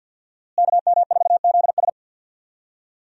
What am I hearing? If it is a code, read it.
Morse code, heard as a keyed tone, about 48 wpm, 700 Hz. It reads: KM4ZS
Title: USA Call Signs - 5 Characters - Repeated 2x - 50wpm